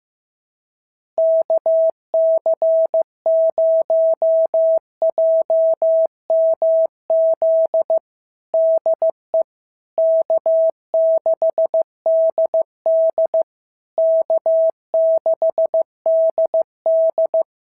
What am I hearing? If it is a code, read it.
Morse code, heard as a keyed tone, about 15 wpm, 650 Hz. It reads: KC0JMZ DE K6DD K6DD
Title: SOTA QSO Practice 15wpm